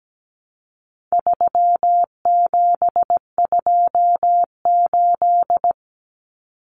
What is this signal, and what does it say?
Morse code, heard as a keyed tone, about 17 wpm, 700 Hz. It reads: 3728
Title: Common Cut Numbers - 4-Digit 17wpm